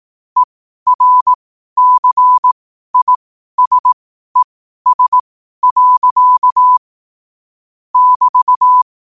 Morse code, heard as a keyed tone, about 18 wpm, spaced as for 12 wpm, 1000 Hz. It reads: ERCISES. <BT>
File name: qotdfiles / 12wpm